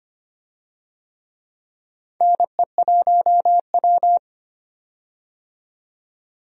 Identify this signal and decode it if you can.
Morse code, heard as a keyed tone, about 25 wpm, 700 Hz. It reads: NE1W
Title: USA Call Signs - 4 Characters - 25wpm